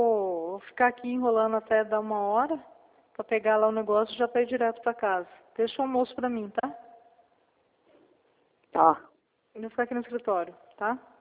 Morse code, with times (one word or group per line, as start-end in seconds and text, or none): none